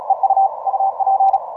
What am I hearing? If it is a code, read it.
Morse code, heard as a keyed tone, about 36 wpm, 780 Hz. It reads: EU UP